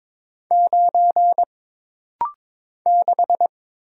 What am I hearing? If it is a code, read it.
Morse code, heard as a keyed tone, about 22 wpm, 700 Hz. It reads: 9 6
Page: This rapid-fire timing is geared to push your ICR (Instant Character Recognition) skills.